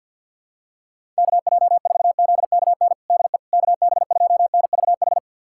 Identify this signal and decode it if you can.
Morse code, heard as a keyed tone, about 47 wpm, 690 Hz. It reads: KJ4ZKN DE KB2NVH
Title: USA Callsign DE Callsign - 6 Characters - 50wpm